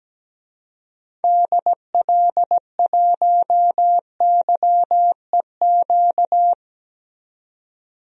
Morse code, heard as a keyed tone, about 17 wpm, 700 Hz. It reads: DL1YEQ
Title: DXCC-100 Call Signs - 6 Characters - 17wpm